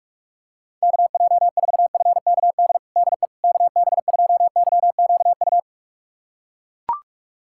Morse code, heard as a keyed tone, about 42 wpm, 700 Hz. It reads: KJ4FKD DE KB2YQU
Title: USA Callsign DE Callsign 45wpm